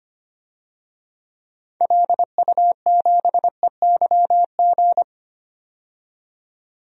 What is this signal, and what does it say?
Morse code, heard as a keyed tone, about 25 wpm, 700 Hz. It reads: LU7EYG
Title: DXCC-100 Call Signs - 6 Characters - 25wpm